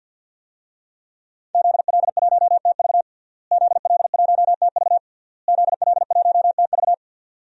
Morse code, heard as a keyed tone, about 49 wpm, 700 Hz. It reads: ZL1TV ZL1TV ZL1TV
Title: DXCC-100 Call Signs - 5 Characters - Repeated 3x - 50wpm